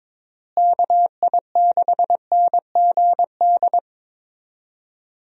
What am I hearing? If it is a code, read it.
Morse code, heard as a keyed tone, about 22 wpm, 700 Hz. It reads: KI6NGD
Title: USA Call Signs - 6 Characters - Repeated 3x - 22wpm